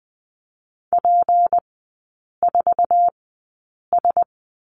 Morse code, heard as a keyed tone, about 20 wpm, spaced as for 8 wpm, 690 Hz. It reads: P4S
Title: 166 Lesson 27e - Number 7 - Three Characters 20wpm